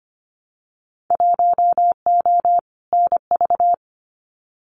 Morse code, heard as a keyed tone, about 25 wpm, 700 Hz. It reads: 1O NV
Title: Winter Field Day Exchanges Encore 25wpm